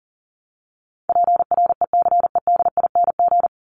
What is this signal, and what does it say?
Morse code, heard as a keyed tone, about 40 wpm, 700 Hz. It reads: PRECEDING
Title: ING Suffix - Top 100 Words with Base Words 40wpm